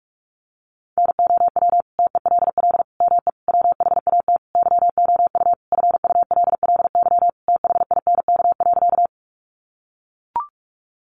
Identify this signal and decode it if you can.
Morse code, heard as a keyed tone, about 44 wpm, 700 Hz. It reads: NOW TELL ME WHAT YOU FULLY THINK.
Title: Sentences from Top 1300 Words 45wpm